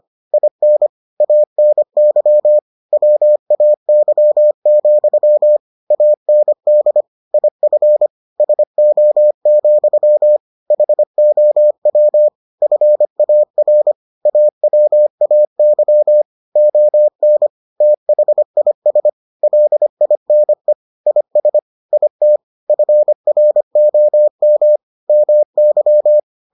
Morse code, heard as a keyed tone, about 25 wpm, 600 Hz. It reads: IN ANY WAY, AND IF SO, HOW FAR AWAY ON THIS LINE IS IT FROM MY